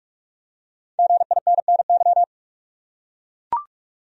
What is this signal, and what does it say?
Morse code, heard as a keyed tone, about 44 wpm, 700 Hz. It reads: GINNY
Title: USA Common Names - Max 5 Letters - 45wpm